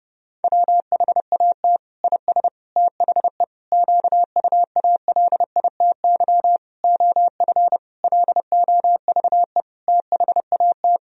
Morse code, heard as a keyed tone, about 32 wpm, 700 Hz. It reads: WHAT IS THE QUALITY OF LOVE THAT
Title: Sentences from Top 1000 Words - Review Part 4 - 30wpm